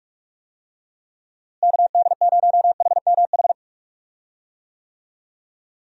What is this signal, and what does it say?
Morse code, heard as a keyed tone, about 46 wpm, 700 Hz. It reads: KD0HMH
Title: USA Call Signs - 6 Characters - Repeated 2x - 45wpm